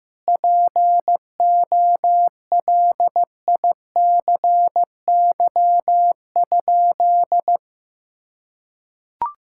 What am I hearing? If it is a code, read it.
Morse code, heard as a keyed tone, about 15 wpm, 700 Hz. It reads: POLICY?
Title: Sentences from Top 1000 Words - Review Part 5 - 15wpm